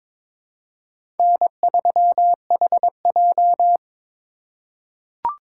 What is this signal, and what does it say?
Morse code, heard as a keyed tone, about 22 wpm, 700 Hz. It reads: N3HJ
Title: USA Call Signs - 4 Characters - 4x Character Spacing - 22wpm